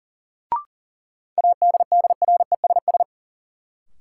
Morse code, heard as a keyed tone, about 40 wpm, 700 Hz. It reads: ADDRESS
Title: Sets of 5 Words Encore - New Word Drill - 40wpm